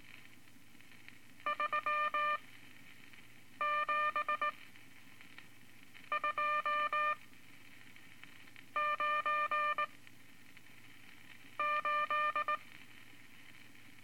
Morse code, heard as a keyed tone, about 19 wpm, spaced as for 5 wpm, 1200 Hz. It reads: 37298
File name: Morse Code Records / 14B LCM 157 Units V Groups of Five